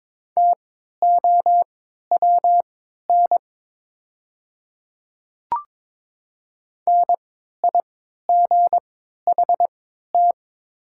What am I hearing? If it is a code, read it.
Morse code, heard as a keyed tone, about 23 wpm, spaced as for 13 wpm, 690 Hz. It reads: TOWN NIGHT
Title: 3x Character Spacing - Top 500 Words - 22wpm